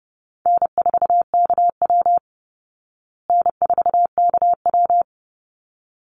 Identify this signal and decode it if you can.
Morse code, heard as a keyed tone, about 30 wpm, 700 Hz. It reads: N4KW N4KW
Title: Call Signs Repeated 3x - 30wpm